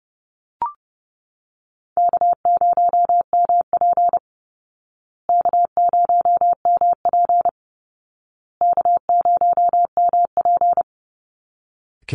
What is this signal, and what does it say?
Morse code, heard as a keyed tone, about 30 wpm, 700 Hz. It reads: K0MP K0MP K0MP
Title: Easy Call Signs Repeated 3x - 30wpm